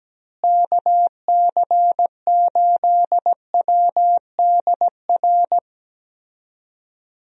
Morse code, heard as a keyed tone, about 17 wpm, 700 Hz. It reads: KC8WDR
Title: USA Call Signs - 6 Characters - Repeated 2x - 17wpm